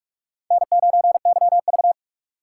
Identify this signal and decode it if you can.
Morse code, heard as a keyed tone, about 42 wpm, 700 Hz. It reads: N9YV
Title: USA Call Signs - 4 Characters - 6x Character Spacing - 45wpm